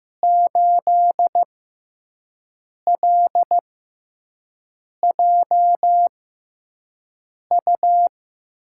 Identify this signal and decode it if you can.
Morse code, heard as a keyed tone, about 15 wpm, spaced as for 5 wpm, 700 Hz. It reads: 8LJU
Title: USA Call Signs - 6 Characters - 6x Character Spacing - 15wpm